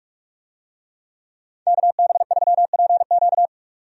Morse code, heard as a keyed tone, about 46 wpm, 700 Hz. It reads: KB3PQ
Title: USA Call Signs - 5 Characters - 45wpm